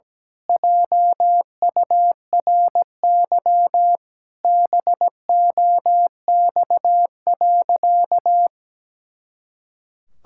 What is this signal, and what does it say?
Morse code, heard as a keyed tone, about 17 wpm, 700 Hz. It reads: JURY BOX.
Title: Pangram Challenge v2 - 17wpm